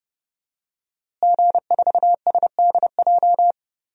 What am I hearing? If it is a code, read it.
Morse code, heard as a keyed tone, about 29 wpm, 700 Hz. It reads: G4SDJ
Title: DXCC-100 Call Signs - 5 Characters - Repeated 2x - 30wpm